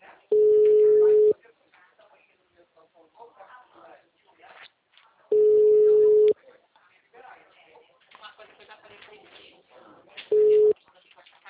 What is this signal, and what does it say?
Ringback tone (Europe, 425 Hz), with speech.